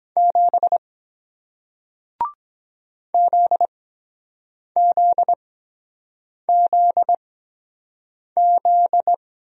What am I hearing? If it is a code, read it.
Morse code, heard as a keyed tone, about 23 wpm, spaced as for 7 wpm, 700 Hz. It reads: 7 ZZZZ